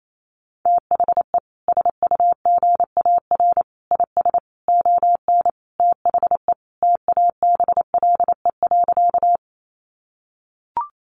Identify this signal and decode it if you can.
Morse code, heard as a keyed tone, about 28 wpm, 700 Hz. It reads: THE SUGAR IS ON THE TABLE.
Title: Sentences from Top 1200 Words 28wpm